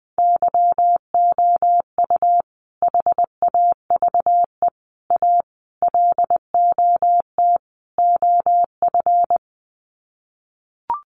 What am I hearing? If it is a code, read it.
Morse code, heard as a keyed tone, about 20 wpm, 700 Hz. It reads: YOU HAVE A LOT OF
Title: Sets of 5 Words Encore 20wpm